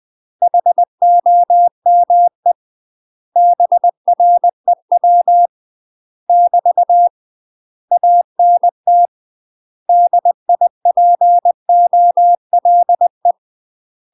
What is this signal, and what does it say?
Morse code, heard as a keyed tone, about 20 wpm, 700 Hz. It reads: HOME BREW <BT> ANT DIPOLE
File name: UK MASTER QSOs TRAINING / 20WPM / UK MASTER QSOs 0068